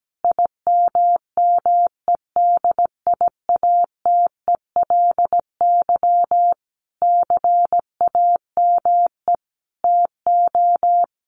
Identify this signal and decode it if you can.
Morse code, heard as a keyed tone, about 17 wpm, 700 Hz. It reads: IMMEDIATELY CAME TO
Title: Sentences from Top 1100 Words 17wpm